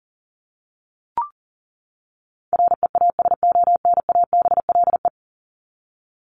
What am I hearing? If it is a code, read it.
Morse code, heard as a keyed tone, about 40 wpm, 700 Hz. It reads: REASONABLE
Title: Top 1501-1750 Words 40wpm